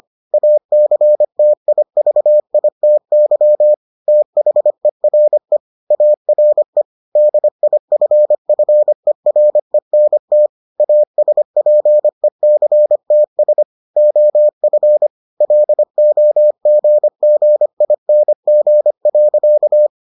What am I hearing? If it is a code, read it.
Morse code, heard as a keyed tone, about 25 wpm, 600 Hz. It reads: ACTIVITY THERE ARE DIFFERENT ASPECTS OF LOGGING.